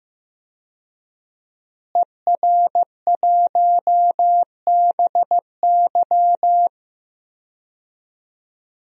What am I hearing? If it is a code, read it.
Morse code, heard as a keyed tone, about 15 wpm, 700 Hz. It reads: ER1BY
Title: DXCC-100 Call Signs - 5 Characters - Repeated 2x - 15wpm